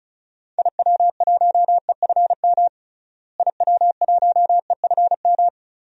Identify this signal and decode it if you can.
Morse code, heard as a keyed tone, about 34 wpm, 700 Hz. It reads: IW1EFM IW1EFM
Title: DXCC-100 Call Signs - 6 Characters - Repeated 2x - 35wpm